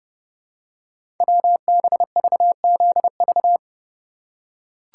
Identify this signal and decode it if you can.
Morse code, heard as a keyed tone, about 30 wpm, 700 Hz. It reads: WBVZV
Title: Code Groups - 5 Letters - 30wpm